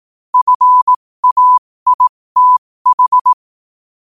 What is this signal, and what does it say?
Morse code, heard as a keyed tone, about 18 wpm, spaced as for 15 wpm, 1000 Hz. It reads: FAITH
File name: qotdfiles / 15wpm